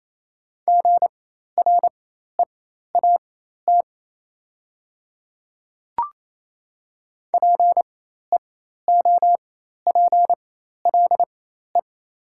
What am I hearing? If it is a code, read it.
Morse code, heard as a keyed tone, about 28 wpm, spaced as for 13 wpm, 700 Hz. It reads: GREAT PEOPLE
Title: 4x Character Spacing - Top 500 Words - 28wpm